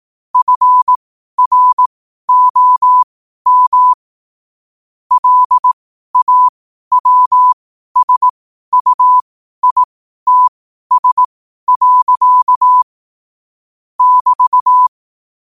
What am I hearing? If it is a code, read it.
Morse code, heard as a keyed tone, about 18 wpm, spaced as for 12 wpm, 1000 Hz. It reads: FROM LAWSUITS. <BT>
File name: qotdfiles / 12wpm